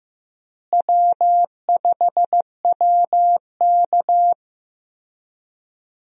Morse code, heard as a keyed tone, about 15 wpm, 700 Hz. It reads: W5WK
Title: USA Call Signs - 4 Characters - Repeated 2x - 15wpm